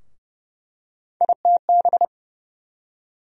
Morse code, heard as a keyed tone, about 31 wpm, 700 Hz. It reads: ITB
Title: USA Call Sign Postfixes - 3 Letters 30wpm